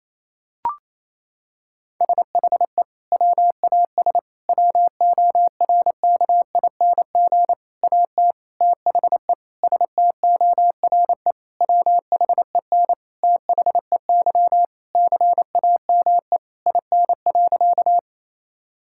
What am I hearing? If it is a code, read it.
Morse code, heard as a keyed tone, about 28 wpm, 700 Hz. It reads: SHE WAS WORKING AT THE STORE WHEN THEY CAME IN.
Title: Sentences from Top 1000 Words - Review Part 5 - 28wpm